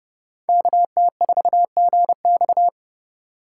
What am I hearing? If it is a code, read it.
Morse code, heard as a keyed tone, about 34 wpm, 700 Hz. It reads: KT4GX
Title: Speed Racing - USA 2x2 Call Signs 30wpm